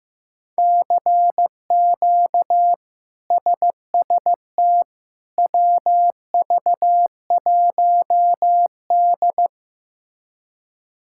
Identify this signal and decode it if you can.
Morse code, heard as a keyed tone, about 15 wpm, 700 Hz. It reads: CQ SST WV1D
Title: K1USN SST QSO Practice - 15wpm